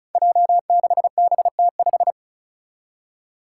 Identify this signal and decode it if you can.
Morse code, heard as a keyed tone, about 34 wpm, 700 Hz. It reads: J6BT5
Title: Five Challenging Letter-Number Combinations 35wpm